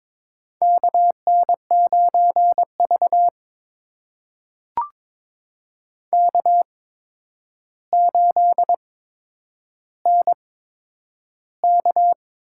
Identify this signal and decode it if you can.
Morse code, heard as a keyed tone, about 22 wpm, spaced as for 9 wpm, 700 Hz. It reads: KN9V K8NK
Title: USA Call Signs - 8x Character Spacing - 22wpm